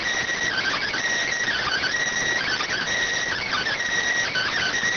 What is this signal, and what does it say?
Emergency Alert System (SAME) tones (header bursts), heard as harsh buzzing bursts. It is heard from 0.5 to 4.7 s.